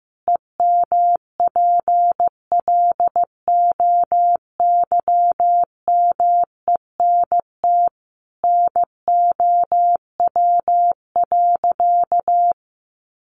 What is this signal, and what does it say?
Morse code, heard as a keyed tone, about 15 wpm, 700 Hz. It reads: EMPLOYMENT NOW.
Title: Sentences from Top 1500 Words 15wpm